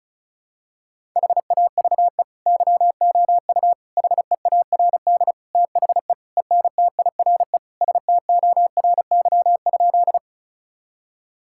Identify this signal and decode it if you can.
Morse code, heard as a keyed tone, about 36 wpm, 700 Hz. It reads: HAVE YOU HEARD THE ENTIRE STORY?
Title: Sentences from Top 1000 Words - Review Part 5 - 35wpm